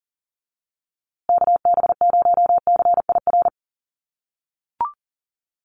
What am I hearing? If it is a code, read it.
Morse code, heard as a keyed tone, about 40 wpm, 700 Hz. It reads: KB0CIR
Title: USA Call Signs Repeated 3x 40wpm